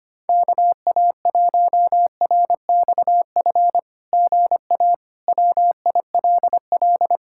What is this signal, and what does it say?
Morse code, heard as a keyed tone, about 25 wpm, 700 Hz. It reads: KA1RXF GA WILL
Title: SOTA QSO Practice 25wpm